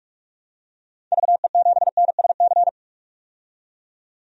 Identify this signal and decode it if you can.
Morse code, heard as a keyed tone, about 46 wpm, 700 Hz. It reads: VE7NSC
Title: DXCC-100 Call Signs - 6 Characters - 45wpm